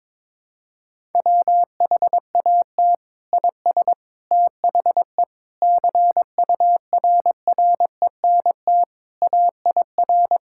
Morse code, heard as a keyed tone, about 22 wpm, 700 Hz. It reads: WHAT IS THE CURRENT AIR
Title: Sentences from Top 1000 Words - Review Part 6 - 22wpm